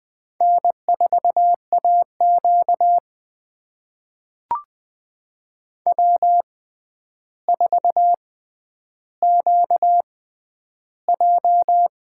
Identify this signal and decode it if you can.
Morse code, heard as a keyed tone, about 21 wpm, spaced as for 10 wpm, 700 Hz. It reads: N4AQ W4QJ
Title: USA Call Signs - 4 Characters - 6x Character Spacing - 20wpm